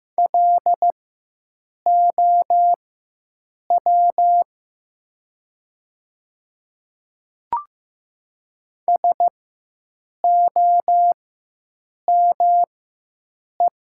Morse code, heard as a keyed tone, about 15 wpm, spaced as for 7 wpm, 700 Hz. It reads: LOW SOME
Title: 4x Character Spacing - Top 200 Words - 15wpm